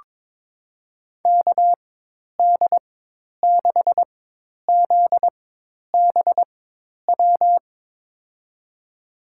Morse code, heard as a keyed tone, about 22 wpm, spaced as for 10 wpm, 700 Hz. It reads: KD6ZBW
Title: USA Call Signs - 4x Character Spacing - 22wpm